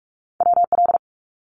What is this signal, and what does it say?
Morse code, heard as a keyed tone, about 46 wpm, 700 Hz. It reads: WL
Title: Abbreviations 45wpm